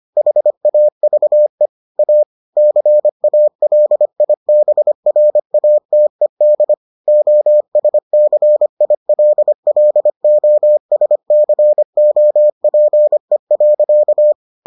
Morse code, heard as a keyed tone, about 25 wpm, 600 Hz. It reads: HAVE A CALIBRATED OSCILLOSCOPE.